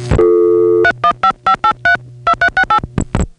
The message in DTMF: 3155832368